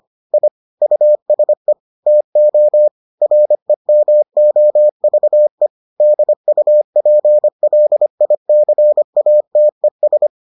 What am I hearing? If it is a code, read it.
Morse code, heard as a keyed tone, about 25 wpm, 600 Hz. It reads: I USE TO REMOVE DUPLICATES